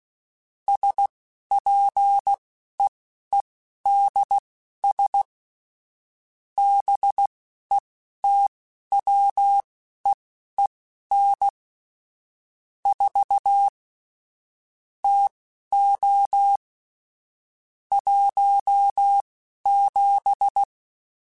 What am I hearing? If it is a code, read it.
Morse code, heard as a keyed tone, about 16 wpm, spaced as for 11 wpm, 800 Hz. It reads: SPEEDS BETWEEN 4 TO 17
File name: Weather Sets / Weather Sets-mp3s / Weather 4 21 @10